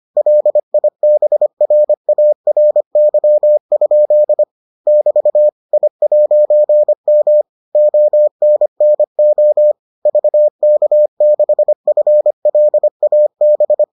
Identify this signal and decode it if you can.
Morse code, heard as a keyed tone, about 25 wpm, 600 Hz. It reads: LIBRARY? <BT> I'M ONNO VK6FLAB